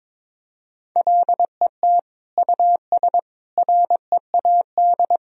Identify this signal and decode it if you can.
Morse code, heard as a keyed tone, about 22 wpm, 700 Hz. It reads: LET US READ
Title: Sentences from Top 1000 Words - Review Part 1 - 22wpm